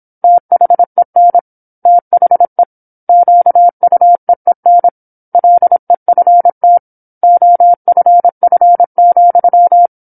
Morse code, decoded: THEN THE QUEEN LEFT OFF,